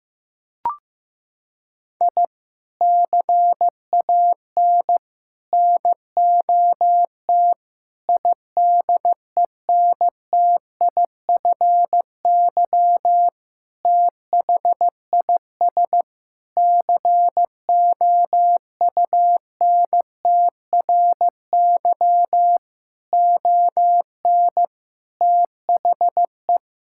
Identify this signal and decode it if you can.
Morse code, heard as a keyed tone, about 15 wpm, 700 Hz. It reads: I CAN NOT IDENTIFY THIS COUNTRY ON THE
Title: Sentences from Top 1000 Words - Review Part 6 - 15wpm